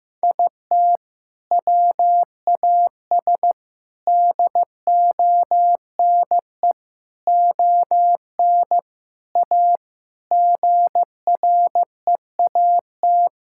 Morse code, heard as a keyed tone, about 15 wpm, 700 Hz. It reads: IT WAS DONE ON A GREAT
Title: Sentences from Top 1000 Words - Review Part 5 - 15wpm